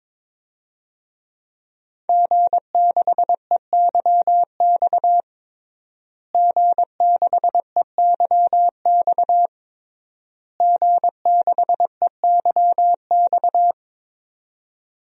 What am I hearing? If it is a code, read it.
Morse code, heard as a keyed tone, about 22 wpm, 700 Hz. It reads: G6EYX G6EYX G6EYX
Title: DXCC-100 Call Signs - 5 Characters - Repeated 3x - 22wpm